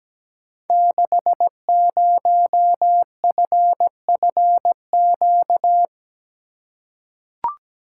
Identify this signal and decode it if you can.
Morse code, heard as a keyed tone, about 17 wpm, 700 Hz. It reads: 60FFQ
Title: Code Groups - 5 Letter:Numbers - 17wpm